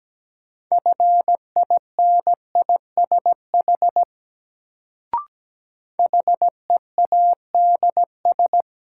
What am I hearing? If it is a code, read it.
Morse code, heard as a keyed tone, about 17 wpm, 700 Hz. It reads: FINISH HEADS
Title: Sentences from Top 1000 Words - Review Part 3 - New Word Drill - 17wpm